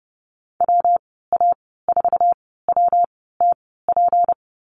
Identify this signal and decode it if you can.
Morse code, heard as a keyed tone, about 30 wpm, spaced as for 17 wpm, 700 Hz. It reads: WA4WTP
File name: Lesson 37g - Letter x - Call Signs 30wpm